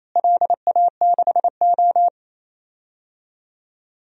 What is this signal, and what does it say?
Morse code, heard as a keyed tone, about 28 wpm, 700 Hz. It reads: LA6O
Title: DXCC-100 Call Signs - 4 Characters - Repeated 3x - 28wpm